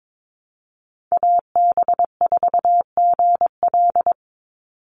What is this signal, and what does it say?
Morse code, heard as a keyed tone, about 22 wpm, 700 Hz. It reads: AB4GL
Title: USA Call Signs 22wpm